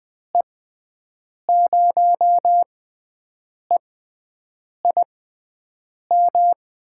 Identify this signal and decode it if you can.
Morse code, heard as a keyed tone, about 20 wpm, spaced as for 7 wpm, 700 Hz. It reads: E0EIM
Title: USA Call Signs - 6 Characters - 6x Character Spacing - 20wpm